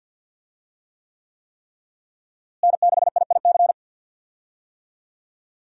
Morse code, heard as a keyed tone, about 51 wpm, 700 Hz. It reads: N6IIC
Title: USA Call Signs - 5 Characters - Repeated 3x - 50wpm